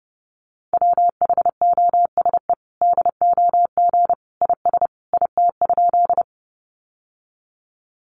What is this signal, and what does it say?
Morse code, heard as a keyed tone, about 30 wpm, 700 Hz. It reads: WHOSE DOG IS IT?
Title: Sentences from Top 1200 Words 30wpm